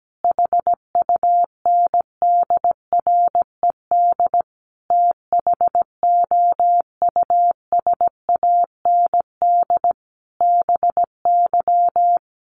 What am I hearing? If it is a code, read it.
Morse code, heard as a keyed tone, about 17 wpm, 700 Hz. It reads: HUNDRED THOUSAND BY
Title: Sentences from Top 1100 Words 17wpm